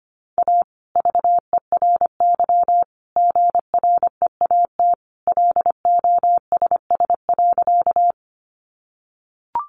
Morse code, heard as a keyed tone, about 25 wpm, 700 Hz. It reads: A VERY GREAT LOSS.
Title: Sentences from Top 1400 Words 25wpm